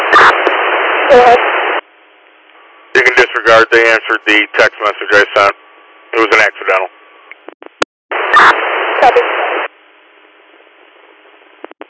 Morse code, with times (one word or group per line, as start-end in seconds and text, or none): none